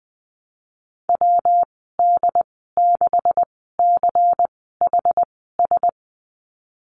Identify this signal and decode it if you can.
Morse code, heard as a keyed tone, about 19 wpm, spaced as for 14 wpm, 700 Hz. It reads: WD6CHS
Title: 147 Lesson 24g - Letter y - Call Signs 20wpm